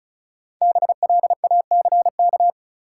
Morse code, heard as a keyed tone, about 36 wpm, 700 Hz. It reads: BLACK
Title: Sets of 5 Words Encore - New Word Drill - 35wpm